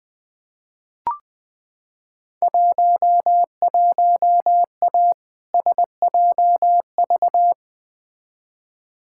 Morse code, heard as a keyed tone, about 20 wpm, 700 Hz. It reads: 11A SJV
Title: Field Day Exchanges Encore - 20wpm